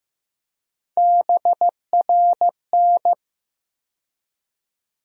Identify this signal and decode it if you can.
Morse code, heard as a keyed tone, about 15 wpm, 700 Hz. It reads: BRN
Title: USA Call Sign Postfixes - 3 Letters 15wpm